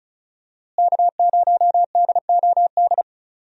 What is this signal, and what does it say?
Morse code, heard as a keyed tone, about 34 wpm, 700 Hz. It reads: K0DOD
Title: USA Call Signs - 5 Characters - Repeated 3x - 35wpm